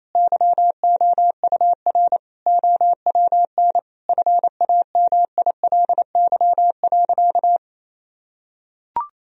Morse code, heard as a keyed tone, about 28 wpm, 700 Hz. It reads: YOUR OWN FAMILY.
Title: Sentences from Top 1000 Words - Review Part 1 - 28wpm